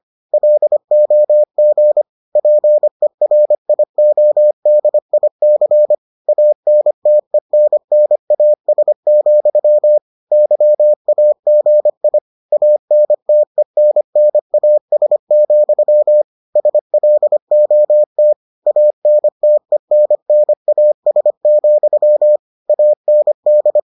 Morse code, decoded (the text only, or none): LOG PERIODIC ANTENNAS, YAGI ANTENNAS, SLOT ANTENNAS, AND